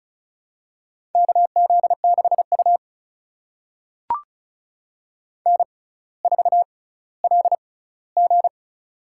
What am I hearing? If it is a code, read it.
Morse code, heard as a keyed tone, about 36 wpm, spaced as for 18 wpm, 700 Hz. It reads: KZ6U N4LG
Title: USA Call Signs - 4 Characters - 6x Character Spacing - 35wpm